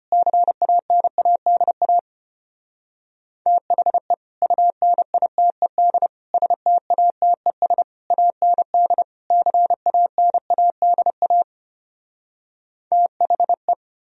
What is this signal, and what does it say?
Morse code, heard as a keyed tone, about 33 wpm, 700 Hz. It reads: CANADA THE UNITED STATES AND CANADA THE